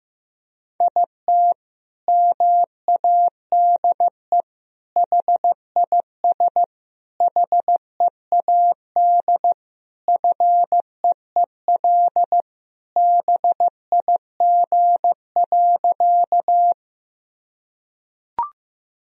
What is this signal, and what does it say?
Morse code, heard as a keyed tone, about 15 wpm, 700 Hz. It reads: IT MADE HIS HEAD FEEL BIG.
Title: Sentences from Top 1000 Words - Review Part 1 - 15wpm